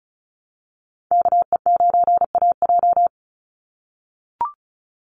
Morse code, heard as a keyed tone, about 34 wpm, 700 Hz. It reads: KE9AJ
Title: NA SOTA Chaser Callsigns 35wpm